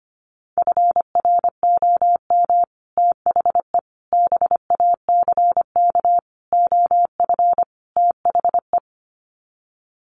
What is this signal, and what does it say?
Morse code, heard as a keyed tone, about 25 wpm, 700 Hz. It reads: FROM THE BACK OF THE
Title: Sets of 5 Words Encore 25wpm